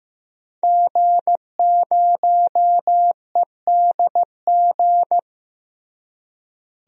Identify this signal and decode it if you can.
Morse code, heard as a keyed tone, about 15 wpm, 700 Hz. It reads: G0EDG
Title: DXCC-100 Call Signs - 5 Characters - Repeated 2x - 15wpm